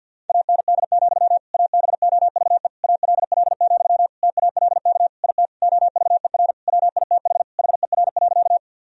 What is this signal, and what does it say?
Morse code, heard as a keyed tone, about 52 wpm, 700 Hz. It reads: AND, ABOVE ALL, TALK IT OVER WITH HER.
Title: Sentences from Top 1000 Words - Review Part 2 - 50wpm